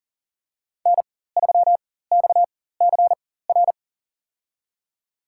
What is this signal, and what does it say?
Morse code, heard as a keyed tone, about 40 wpm, spaced as for 19 wpm, 700 Hz. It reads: N3XCR
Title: USA Call Signs - 5 Characters - 4x Character Spacing - 40wpm